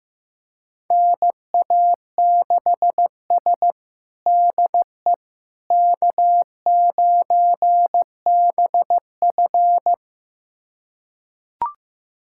Morse code, heard as a keyed tone, about 15 wpm, 700 Hz. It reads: NA6S DE K9BF
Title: USA Callsign DE Callsign - 4 Characters - 15wpm